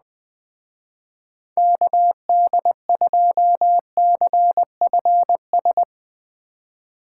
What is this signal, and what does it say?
Morse code, heard as a keyed tone, about 20 wpm, 700 Hz. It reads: KD2CFS
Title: Speed Racing - USA 2x3 Call Signs 17wpm